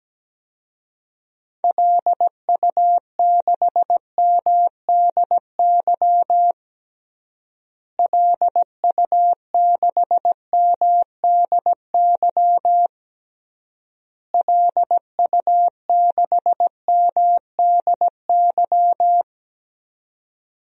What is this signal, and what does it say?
Morse code, heard as a keyed tone, about 17 wpm, 700 Hz. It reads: LU6MDY LU6MDY LU6MDY
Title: DXCC-100 Call Signs - 6 Characters - Repeated 3x - 17wpm